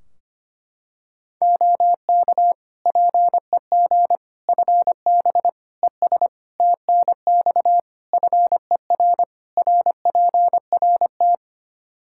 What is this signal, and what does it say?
Morse code, heard as a keyed tone, about 25 wpm, 700 Hz. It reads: OK PEG FB ES TNX FER RPRT
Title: Practice QSO - Exchange 2 - Info - 25wpm